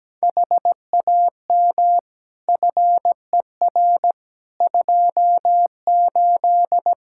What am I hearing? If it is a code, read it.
Morse code, heard as a keyed tone, about 17 wpm, 700 Hz. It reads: HAM FER 28
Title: Practice QSO - Exchange 3 - More Info - 17wpm